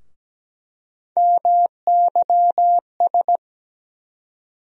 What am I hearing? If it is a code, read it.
Morse code, heard as a keyed tone, about 17 wpm, 700 Hz. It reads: MYS
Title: USA Call Sign Postfixes - 3 Letters 17wpm